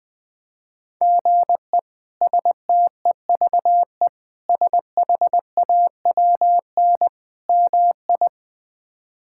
Morse code, heard as a keyed tone, about 20 wpm, 700 Hz. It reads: GE STEVE SHAWN MI
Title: K1USN SST QSO Practice - 20wpm